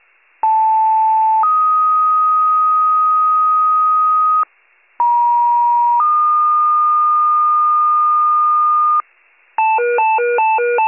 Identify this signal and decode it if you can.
Two-tone page, heard as a one-second tone, then a three-second tone, about 880 then 1250 Hz, about 950 then 1220 Hz. It is heard from 0.4 to 9.0 s.